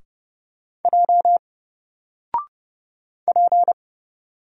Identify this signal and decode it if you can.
Morse code, heard as a keyed tone, about 31 wpm, 700 Hz. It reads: J P
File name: Single-Letter-Number-30wpm